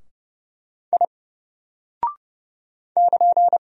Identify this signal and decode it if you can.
Morse code, heard as a keyed tone, about 30 wpm, 700 Hz. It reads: I <KN>
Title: Single Character (Letter:Number:Punctuation:Prosign) Alt 30wpm